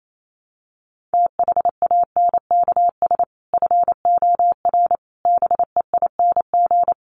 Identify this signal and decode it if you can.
Morse code, heard as a keyed tone, about 28 wpm, 700 Hz. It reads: THANKS FOR BEING
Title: Sets of 5 Words 28wpm